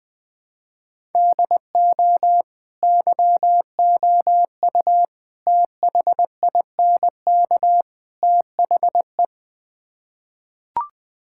Morse code, heard as a keyed tone, about 20 wpm, 700 Hz. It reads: DO YOU THINK THE
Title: Build A Phrase - 4-Word Phrases - 20wpm